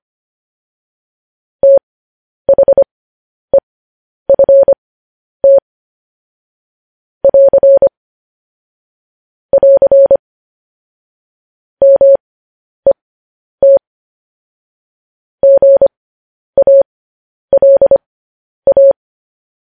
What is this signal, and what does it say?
Morse code, heard as a keyed tone, about 25 wpm, spaced as for 10 wpm, 550 Hz. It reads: THEFT <AR> <AR> MET GALA